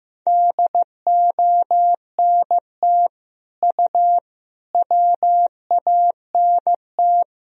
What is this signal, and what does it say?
Morse code, heard as a keyed tone, about 15 wpm, 700 Hz. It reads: DONT U WANT
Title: Sentences from Top 300 Words and Abbreviations 15wpm